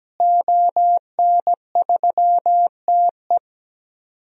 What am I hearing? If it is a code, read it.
Morse code, heard as a keyed tone, about 17 wpm, 700 Hz. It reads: ON3TE
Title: DXCC-100 Call Signs - 5 Characters - Repeated 2x - 17wpm